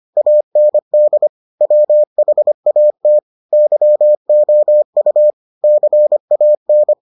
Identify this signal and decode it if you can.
Morse code, heard as a keyed tone, about 25 wpm, 600 Hz. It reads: AND WHAT YOU CAN